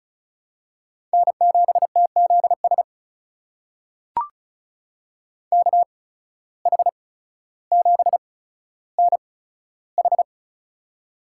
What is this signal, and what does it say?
Morse code, heard as a keyed tone, about 36 wpm, spaced as for 15 wpm, 700 Hz. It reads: N7TZS KH7NH
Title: USA Call Signs - 5 Characters - 8x Character Spacing - 35wpm